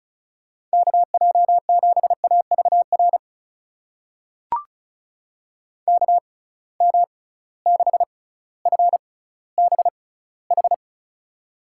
Morse code, heard as a keyed tone, about 36 wpm, spaced as for 18 wpm, 690 Hz. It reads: KJ7AVR KM6FBH
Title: USA Call Signs - 6 Characters - 6x Character Spacing - 35wpm